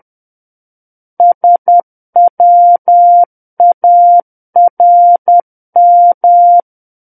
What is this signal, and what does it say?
Morse code, heard as a keyed tone, about 10 wpm, 700 Hz. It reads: SWARM
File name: Book One / 01-Chapter 1 - 10 WPM